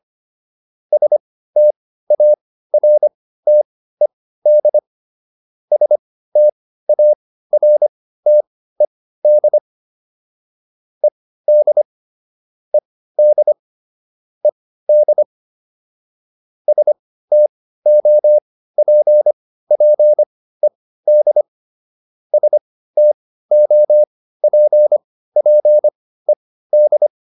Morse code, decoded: STARTED STARTED ED ED ED STOPPED STOPPED

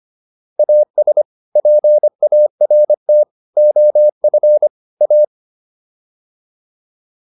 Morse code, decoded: AS PART OF A

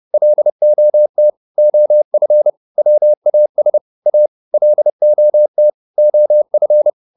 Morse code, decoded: LOT OF WAS A LOT OF

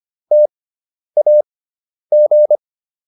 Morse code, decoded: TAG

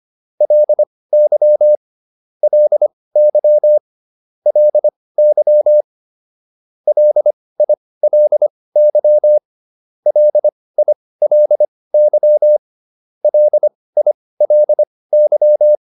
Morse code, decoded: LY LY LY LILY LILY LILY

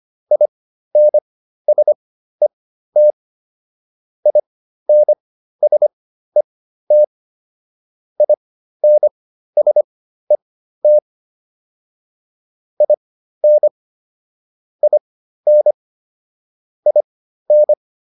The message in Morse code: INSET INSET INSET IN IN IN

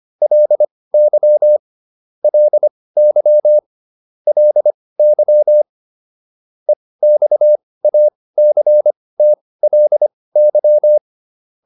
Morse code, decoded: LY LY LY EXACTLY

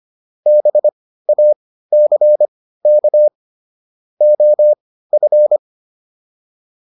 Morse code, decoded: BACK OF

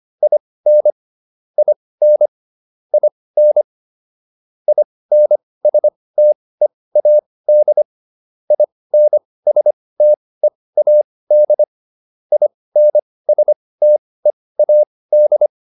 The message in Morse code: IN IN IN INSTEAD INSTEAD INSTEAD

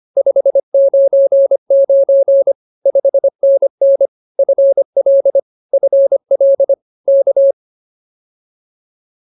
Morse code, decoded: 599 5NN FL FL K

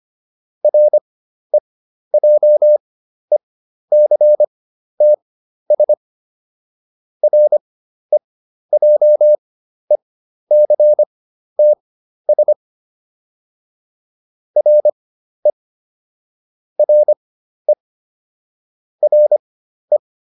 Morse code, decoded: REJECTS REJECTS RE RE RE

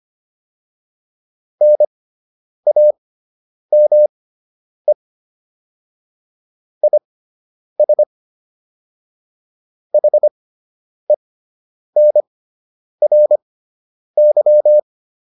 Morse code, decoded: NAME IS HENRY

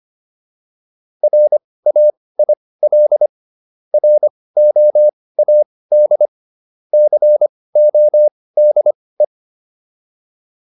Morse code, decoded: RAIL ROAD CODE

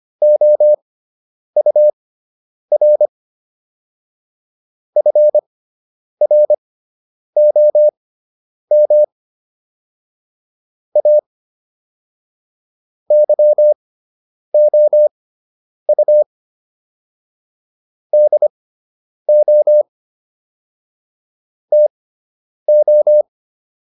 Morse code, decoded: OUR FROM A YOU DO TO

